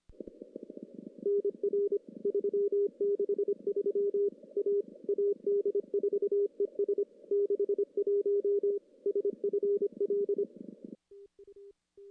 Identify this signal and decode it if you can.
Morse code, heard as a keyed tone, about 27 wpm, 400 Hz. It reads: NR 363 A AD4ES 61 SFL
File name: contests / CW-SS-k5tr- -1600